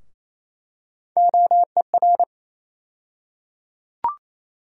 Morse code, heard as a keyed tone, about 27 wpm, 700 Hz. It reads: OER
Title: USA Call Sign Postfixes - 3 Letters 28wpm